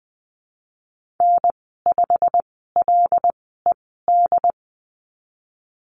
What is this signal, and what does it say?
Morse code, heard as a keyed tone, about 20 wpm, spaced as for 15 wpm, 700 Hz. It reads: N5LED
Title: 122 Lesson 20g - Number 3 - Call Signs 20wpm